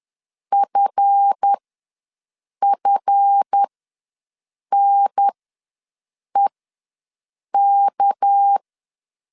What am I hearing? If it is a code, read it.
Morse code, heard as a keyed tone, about 10 wpm, spaced as for 6 wpm, 800 Hz. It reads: FFNEK
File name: LEZIONI / 7 GF9Z / 7 GF ws 10wpm 3918